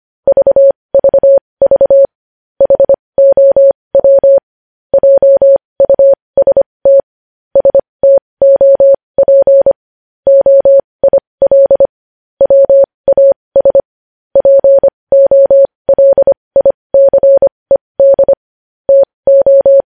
Morse code, heard as a keyed tone, about 25 wpm, spaced as for 20 wpm, 550 Hz. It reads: VVV HOW JUST STOP OIL WAS POLICED TO
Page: news headlines in morse code at 20 wpm